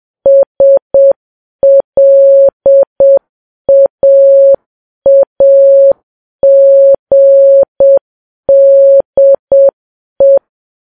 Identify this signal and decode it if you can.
Morse code, heard as a keyed tone, about 7 wpm, 550 Hz. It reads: SLAAGDE
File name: Files Web tls / Morse on9fv / morse7kps